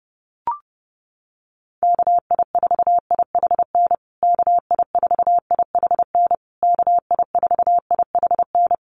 Morse code, heard as a keyed tone, about 28 wpm, 700 Hz. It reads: KI4IHN KI4IHN KI4IHN
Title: USA Call Signs Repeated 3x 30wpm